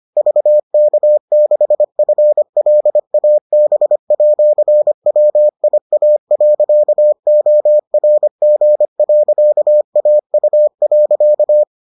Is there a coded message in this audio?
Yes: Morse code from 0.2 to 11.6 s, heard as a keyed tone, about 25 wpm, 600 Hz. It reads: VK6FLAB@WIA.ORG.AU.